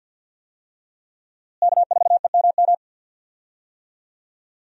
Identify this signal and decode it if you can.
Morse code, heard as a keyed tone, about 48 wpm, 700 Hz. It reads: K4EMM
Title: USA Call Signs - 5 Characters - Repeated 2x - 50wpm